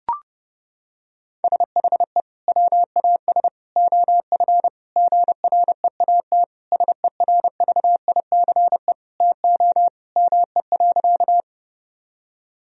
Morse code, heard as a keyed tone, about 30 wpm, 700 Hz. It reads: SHE WAS OF GREAT SERVICE TO ME.
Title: Sentences from Top 1000 Words - Review Part 4 - 30wpm